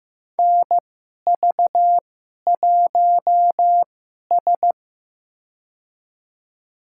Morse code, heard as a keyed tone, about 15 wpm, spaced as for 11 wpm, 700 Hz. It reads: NV1S
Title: USA Call Signs - 4 Characters - 2x Character Spacing - 15wpm